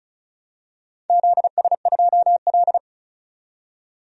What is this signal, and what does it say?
Morse code, heard as a keyed tone, about 36 wpm, 690 Hz. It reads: ZS2L
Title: DXCC-100 Call Signs - 4 Characters - 35wpm